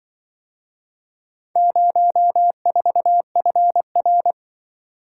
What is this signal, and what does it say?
Morse code, heard as a keyed tone, about 25 wpm, 700 Hz. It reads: 04FR
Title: Speed-Racing - Four Letter-Number Combinations - 20wpm